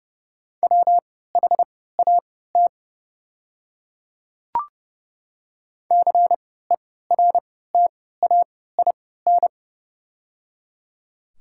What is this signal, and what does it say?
Morse code, heard as a keyed tone, about 30 wpm, spaced as for 17 wpm, 700 Hz. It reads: WHAT CERTAIN
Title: 3x Character Spacing - Top 500 Words - 30wpm